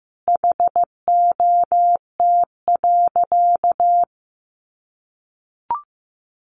Morse code, heard as a keyed tone, about 15 wpm, 700 Hz. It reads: HOT.
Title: Sentences from Top 500 Words 15wpm